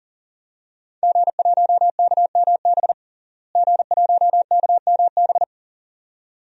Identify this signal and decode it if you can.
Morse code, heard as a keyed tone, about 40 wpm, 700 Hz. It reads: G1KMB G1KMB
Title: DXCC-100 Call Signs - 5 Characters - Repeated 2x - 40wpm